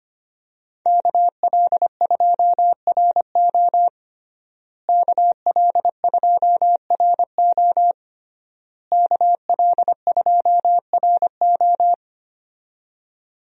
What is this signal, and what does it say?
Morse code, heard as a keyed tone, about 25 wpm, 700 Hz. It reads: KL2RO KL2RO KL2RO
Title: USA Call Signs - 5 Characters - Repeated 3x - 25wpm